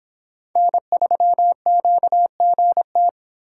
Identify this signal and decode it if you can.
Morse code, heard as a keyed tone, about 26 wpm, 700 Hz. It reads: N3QGT